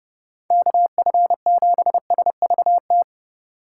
Morse code, heard as a keyed tone, about 34 wpm, 700 Hz. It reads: KF7SVT